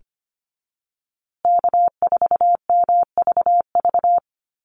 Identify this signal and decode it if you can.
Morse code, heard as a keyed tone, about 25 wpm, 700 Hz. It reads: K4MVV
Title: USA Call Signs Repeated 3x 25wpm